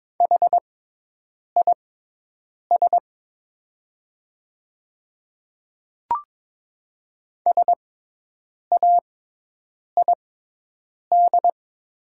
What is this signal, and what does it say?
Morse code, heard as a keyed tone, about 22 wpm, spaced as for 8 wpm, 700 Hz. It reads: HIS SAID
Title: 6x Character Spacing - Top 200 Words - 22wpm